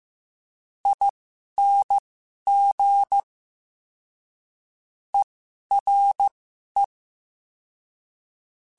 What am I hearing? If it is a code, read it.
Morse code, heard as a keyed tone, about 15 wpm, spaced as for 11 wpm, 800 Hz. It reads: ING ERE
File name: Special INGs-mp3s / Special INGs-mp3s / Ere ing @10